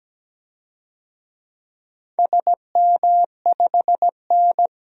Morse code, heard as a keyed tone, about 17 wpm, 700 Hz. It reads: SM5N
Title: DXCC-100 Call Signs - 4 Characters - 17wpm